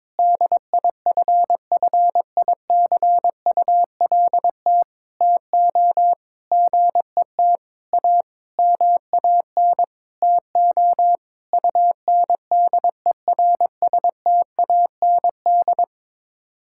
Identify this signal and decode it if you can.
Morse code, heard as a keyed tone, about 22 wpm, 700 Hz. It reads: DIFFICULT TO GET A MAN TO UNDERSTAND
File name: Sets of 9 Words - 22wpm